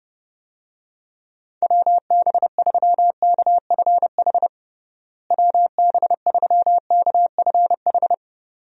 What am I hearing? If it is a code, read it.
Morse code, heard as a keyed tone, about 30 wpm, 700 Hz. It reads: WB3KFH WB3KFH
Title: USA Call Signs - 6 Characters - Repeated 2x - 30wpm